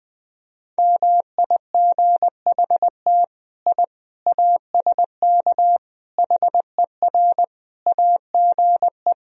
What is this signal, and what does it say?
Morse code, heard as a keyed tone, about 20 wpm, 700 Hz. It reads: MIGHT I ASK HER AGE
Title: Sentences from Top 1000 Words - Review Part 3 - 20wpm